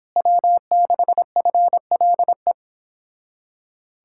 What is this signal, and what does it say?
Morse code, heard as a keyed tone, about 26 wpm, 700 Hz. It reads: W6FLE
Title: Speed-Racing Common QSO Phrases Encore - 22wpm